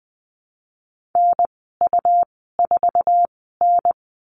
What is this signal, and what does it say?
Morse code, heard as a keyed tone, about 20 wpm, spaced as for 15 wpm, 700 Hz. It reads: NU4N
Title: 167 Lesson 27g - Number 7 - Call Signs 20wpm